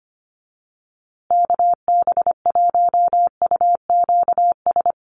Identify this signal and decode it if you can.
Morse code, heard as a keyed tone, about 25 wpm, 700 Hz. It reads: KB1UQS
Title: USA Call Signs Repeated 3x 25wpm